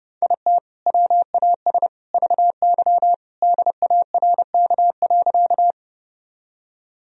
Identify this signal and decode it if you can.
Morse code, heard as a keyed tone, about 30 wpm, 700 Hz. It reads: IT WAS VY DARK.
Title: Sentences from Top 500 Words and Abbreviations 30wpm